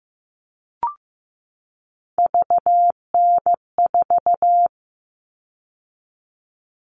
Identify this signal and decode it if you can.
Morse code, heard as a keyed tone, about 15 wpm, 700 Hz. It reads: VN4
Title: All Cut Numbers - 3-Digit 15wpm